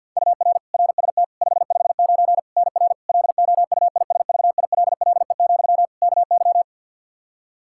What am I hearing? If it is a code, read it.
Morse code, heard as a keyed tone, about 52 wpm, 700 Hz. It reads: UR RST 559 NR LOUISVILLE, KY
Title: K1USN SST QSO Practice 50wpm